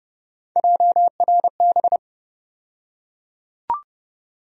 Morse code, heard as a keyed tone, about 30 wpm, 700 Hz. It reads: JRB
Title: USA Call Sign Postfixes - 3 Letters 30wpm